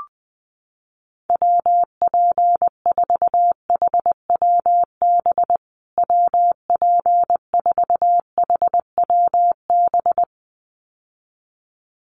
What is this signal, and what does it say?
Morse code, heard as a keyed tone, about 20 wpm, 700 Hz. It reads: WP4HWB WP4HWB
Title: USA Call Signs Repeated 2x 20wpm